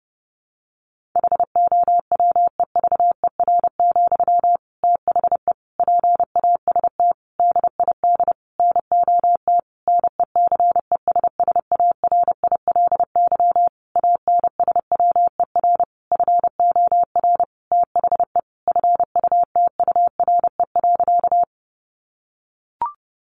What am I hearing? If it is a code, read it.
Morse code, heard as a keyed tone, about 30 wpm, 700 Hz. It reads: HOWEVER, THE PAST DID NOT NECESSARILY ANSWER FOR THE FUTURE.
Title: Sentences from Top 1400 Words 30wpm